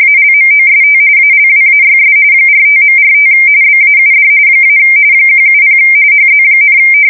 RTTY, about 45.45 baud. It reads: ALMOST ALWAYS 45.45 BAUD - 170 HZ,
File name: fsk170 / teletype-RTTY